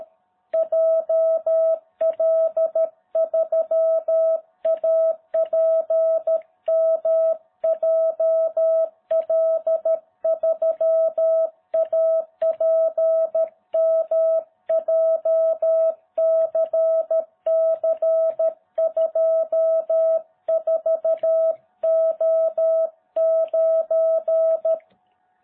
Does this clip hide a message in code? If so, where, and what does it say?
Yes: Morse code from 0.5 to 24.8 s, heard as a keyed tone, about 13 wpm, 660 Hz. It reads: JL3APM JL3APM JCC24O9